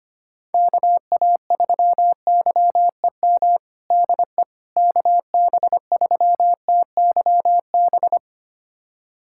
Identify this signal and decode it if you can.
Morse code, heard as a keyed tone, about 25 wpm, 700 Hz. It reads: KA3YEM DE KB3TYB
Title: USA Callsign DE Callsign 25wpm